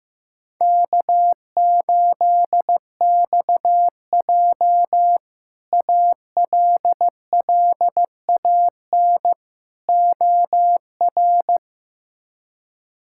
Morse code, heard as a keyed tone, about 15 wpm, 700 Hz. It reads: K8XJ ALLAN OR
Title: K1USN SST QSO Practice - 15wpm